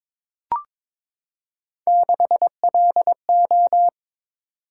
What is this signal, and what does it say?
Morse code, heard as a keyed tone, about 21 wpm, 700 Hz. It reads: 6LO
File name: practice / 22wpm / Three Letter-Number Combinations - 22wpm